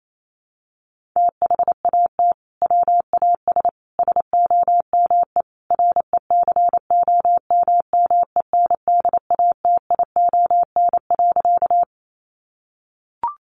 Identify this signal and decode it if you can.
Morse code, heard as a keyed tone, about 28 wpm, 700 Hz. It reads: THAT WAS SOME RECOMMENDATION.
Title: Sentences from Top 1400 Words 28wpm